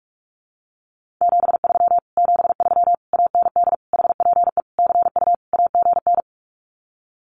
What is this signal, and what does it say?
Morse code, heard as a keyed tone, about 46 wpm, 700 Hz. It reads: 73 73 AND HPE CU AGN
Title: Common QSO Phrases Encore 45wpm